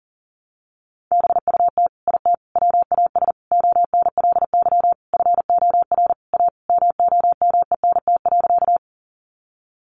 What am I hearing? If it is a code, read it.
Morse code, heard as a keyed tone, about 40 wpm, 700 Hz. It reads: BUT IT WAS ONLY FOR A MOMENT.
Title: Sentences from Top 1000 Words 40wpm